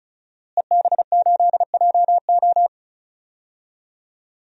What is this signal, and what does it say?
Morse code, heard as a keyed tone, about 36 wpm, 700 Hz. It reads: EB8JO
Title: DXCC-100 Call Signs - 5 Characters - Repeated 2x - 35wpm